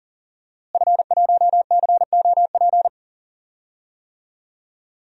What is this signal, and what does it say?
Morse code, heard as a keyed tone, about 40 wpm, 700 Hz. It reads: F1COP
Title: DXCC-100 Call Signs - 5 Characters - Repeated 3x - 40wpm